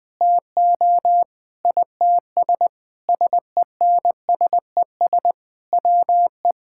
Morse code, heard as a keyed tone, about 20 wpm, 700 Hz. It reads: TO ITS SENSES WE
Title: Sentences with 9-Word Phrases - 20wpm